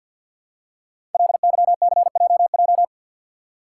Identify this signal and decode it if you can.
Morse code, heard as a keyed tone, about 53 wpm, 700 Hz. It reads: LYCJJ
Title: Code Groups - 5 Letter:Numbers - 50wpm